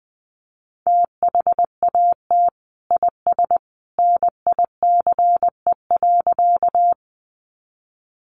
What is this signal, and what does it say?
Morse code, heard as a keyed tone, about 20 wpm, 700 Hz. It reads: THAT IS NICE.
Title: Sentences from Top 1100 Words 20wpm